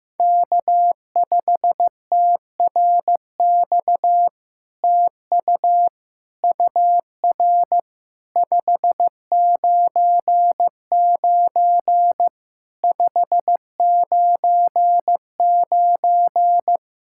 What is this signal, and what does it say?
Morse code, heard as a keyed tone, about 15 wpm, 700 Hz. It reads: K5TRX TU UR 599 599
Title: SOTA QSO Practice 15wpm